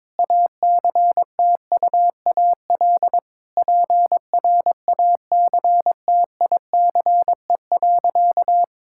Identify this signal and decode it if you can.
Morse code, heard as a keyed tone, about 22 wpm, 700 Hz. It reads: ACTUAL PRACTICE.
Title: Sentences from Top 1000 Words - Review Part 5 - 22wpm